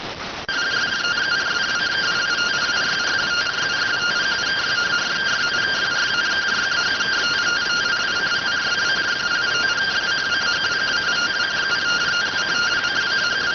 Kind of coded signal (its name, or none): two-tone teleprinter signal